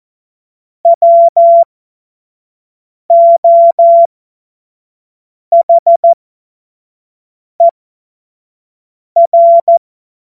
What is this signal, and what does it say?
Morse code, heard as a keyed tone, about 14 wpm, spaced as for 5 wpm, 680 Hz. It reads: WOHER
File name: Morsekurs / Witze / Witz 22. Februar 2022 25 F70